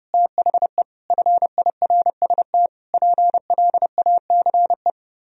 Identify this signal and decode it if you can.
Morse code, heard as a keyed tone, about 30 wpm, 700 Hz. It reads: THE FIRST PLACE